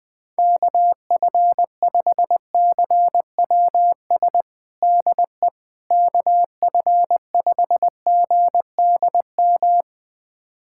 Morse code, heard as a keyed tone, about 20 wpm, 700 Hz. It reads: KF5CWS DE KF5GDM
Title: USA Callsign DE Callsign - 6 Characters - 20wpm